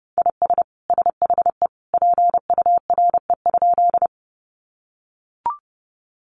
Morse code, heard as a keyed tone, about 30 wpm, 700 Hz. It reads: IS SHE PURE?
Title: Sentences from Top 1500 Words 30wpm